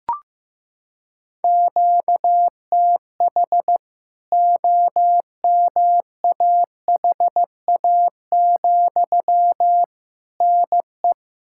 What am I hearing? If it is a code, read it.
Morse code, heard as a keyed tone, about 15 wpm, 700 Hz. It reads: QTH OMAHA, NE
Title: K1USN SST QSO Practice 15wpm